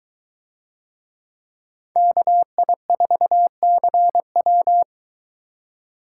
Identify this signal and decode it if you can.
Morse code, heard as a keyed tone, about 22 wpm, 700 Hz. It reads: KI4CW